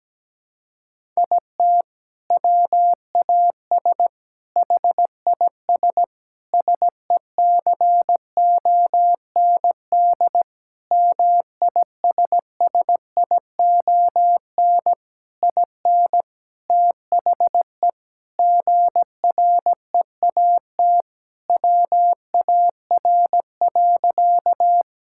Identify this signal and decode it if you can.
Morse code, heard as a keyed tone, about 17 wpm, 700 Hz. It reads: IT WAS HIS SECOND MISSION IN THE GREAT WAR.
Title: Sentences from Top 1600 Words 17wpm